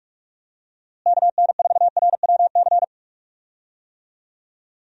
Morse code, heard as a keyed tone, about 44 wpm, 700 Hz. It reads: KN4RWC
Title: USA Call Signs - 6 Characters - Repeated 2x - 45wpm